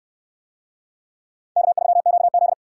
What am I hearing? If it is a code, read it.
Morse code, heard as a keyed tone, about 69 wpm, 700 Hz. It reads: K3QZ